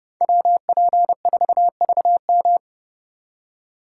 Morse code, heard as a keyed tone, about 29 wpm, 700 Hz. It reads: WP4VM